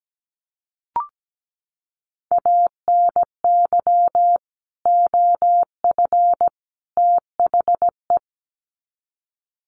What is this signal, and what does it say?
Morse code, heard as a keyed tone, about 17 wpm, 700 Hz. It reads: ANY OF THE
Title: Sets of 3 Words 17wpm